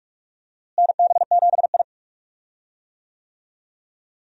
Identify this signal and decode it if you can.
Morse code, heard as a keyed tone, about 44 wpm, 700 Hz. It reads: NB7I
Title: USA Call Signs - 4 Characters - Repeated 3x - 45wpm